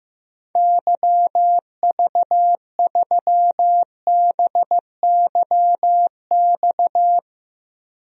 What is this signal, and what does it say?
Morse code, heard as a keyed tone, about 15 wpm, 700 Hz. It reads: YV3BYX
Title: DXCC-100 Call Signs - 6 Characters - Repeated 2x - 15wpm